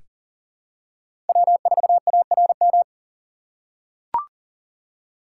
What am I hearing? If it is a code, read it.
Morse code, heard as a keyed tone, about 38 wpm, 700 Hz. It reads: W4ARM
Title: USA Call Signs - 5 Characters - 2x Character Spacing - 40wpm